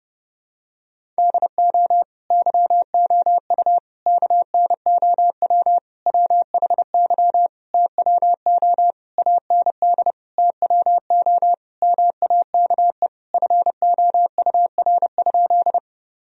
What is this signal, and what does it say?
Morse code, heard as a keyed tone, about 30 wpm, 700 Hz. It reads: DO YOU KNOW WHY TWO AND TWO MAKE FOUR?
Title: Sentences from Top 1000 Words - Review Part 1 - 30wpm